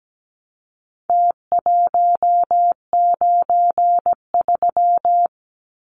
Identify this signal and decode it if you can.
Morse code, heard as a keyed tone, about 17 wpm, 700 Hz. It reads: T193
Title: Numbers - Contest Style - 4-Digit 17wpm